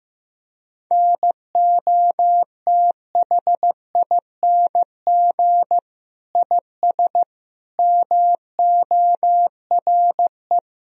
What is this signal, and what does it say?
Morse code, heard as a keyed tone, about 15 wpm, 700 Hz. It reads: NOTHING IS MORE
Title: Sentences from Top 1000 Words - Review Part 4 - 15wpm